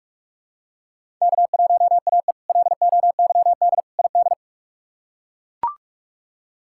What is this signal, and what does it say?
Morse code, heard as a keyed tone, about 41 wpm, 700 Hz. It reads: K1AE LOYD ID
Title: K1USN SST QSO Practice - 45wpm